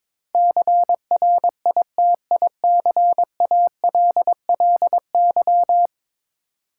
Morse code, heard as a keyed tone, about 22 wpm, 700 Hz. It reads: CRITICALLY
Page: morse-code-ninja-practice-22wpm